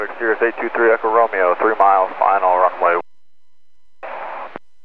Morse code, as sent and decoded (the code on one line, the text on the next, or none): none